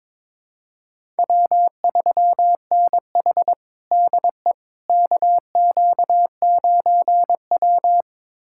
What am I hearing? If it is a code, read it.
Morse code, heard as a keyed tone, about 22 wpm, 700 Hz. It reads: W3NH DE KQ9W
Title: USA Callsign DE Callsign - 4 Characters - 22wpm